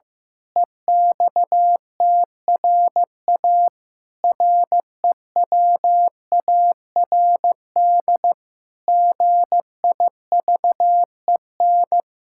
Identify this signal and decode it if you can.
Morse code, heard as a keyed tone, about 16 wpm, 700 Hz. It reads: EXTRA REWARD GIVEN
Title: Pangram Challenge v2 - 15wpm